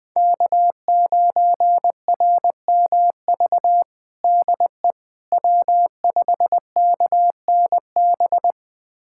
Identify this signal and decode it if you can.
Morse code, heard as a keyed tone, about 20 wpm, 700 Hz. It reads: K9RMV DE W5KNB
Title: USA Callsign DE Callsign - 5 Characters - 20wpm